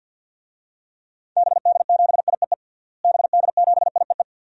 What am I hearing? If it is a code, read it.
Morse code, heard as a keyed tone, about 47 wpm, 700 Hz. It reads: BD7IEE BD7IEE
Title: DXCC-100 Call Signs - 6 Characters - Repeated 2x - 50wpm